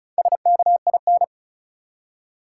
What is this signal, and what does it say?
Morse code, heard as a keyed tone, about 34 wpm, 700 Hz. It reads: SKIN
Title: Four Letter Words - 35wpm